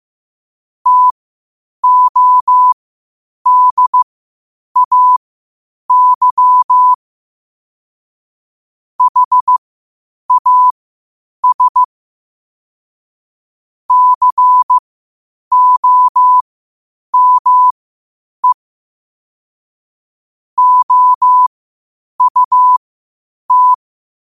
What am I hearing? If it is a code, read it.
Morse code, heard as a keyed tone, about 15 wpm, spaced as for 8 wpm, 1000 Hz. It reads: TODAY HAS COME OUT
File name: qotdfiles / 8wpm